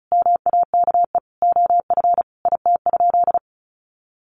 Morse code, heard as a keyed tone, about 34 wpm, 700 Hz. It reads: MAKE OF IT?
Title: Sentences from Top 100 Words 35wpm